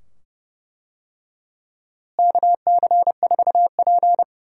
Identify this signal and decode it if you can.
Morse code, heard as a keyed tone, about 32 wpm, 700 Hz. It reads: KC4P
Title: USA Call Signs - 4 Characters - Repeated 2x - 30wpm